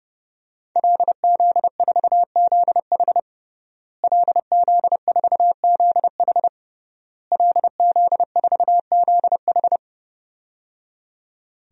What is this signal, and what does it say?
Morse code, heard as a keyed tone, about 30 wpm, 700 Hz. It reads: LZ4ZH LZ4ZH LZ4ZH
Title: DXCC-100 Call Signs - 5 Characters - Repeated 3x - 30wpm